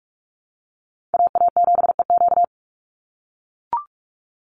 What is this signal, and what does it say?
Morse code, heard as a keyed tone, about 44 wpm, 700 Hz. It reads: AA7EQ
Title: NA SOTA Chaser Callsigns 45wpm